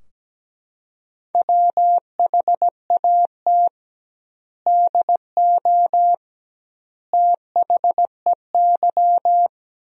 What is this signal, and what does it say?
Morse code, heard as a keyed tone, about 17 wpm, 700 Hz. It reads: WHAT DO THEY
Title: 2x Word-Spacing - Sentences from Top 100 Words - 17wpm